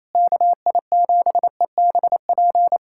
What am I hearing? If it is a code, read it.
Morse code, heard as a keyed tone, about 28 wpm, 700 Hz. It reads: KI7EBP
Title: USA Callsign DE Callsign - 6 Characters - 28wpm